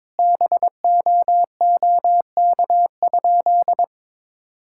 Morse code, heard as a keyed tone, about 22 wpm, 700 Hz. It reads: BOOK?
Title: Sentences from Top 1000 Words - Review Part 5 - 22wpm